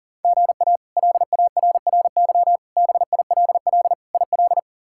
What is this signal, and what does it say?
Morse code, heard as a keyed tone, about 40 wpm, 700 Hz. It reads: GA LARRY BILL IL
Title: K1USN SST QSO Practice - 40wpm